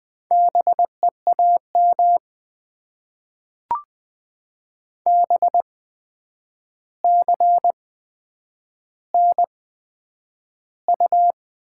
Morse code, heard as a keyed tone, about 20 wpm, spaced as for 9 wpm, 700 Hz. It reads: BEAM BCNU
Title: Top 90 QSO Elements - 8x Character Spacing - 20wpm